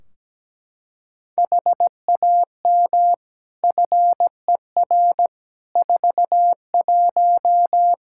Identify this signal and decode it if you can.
Morse code, heard as a keyed tone, about 17 wpm, 700 Hz. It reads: HAM FER 41
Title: Practice QSO - Exchange 3 - More Info - 17wpm